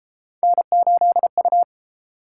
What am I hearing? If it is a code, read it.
Morse code, heard as a keyed tone, about 30 wpm, 700 Hz. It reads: N8U